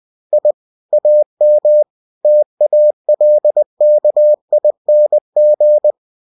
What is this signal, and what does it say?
Morse code, heard as a keyed tone, about 20 wpm, 600 Hz. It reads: I AM TALKING
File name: cw-books / winnie-the-pooh / winnie-20wpm-page-0103